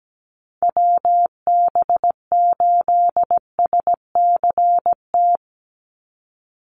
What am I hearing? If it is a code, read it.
Morse code, heard as a keyed tone, about 17 wpm, 700 Hz. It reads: WB8SCT
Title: Call Signs - 17wpm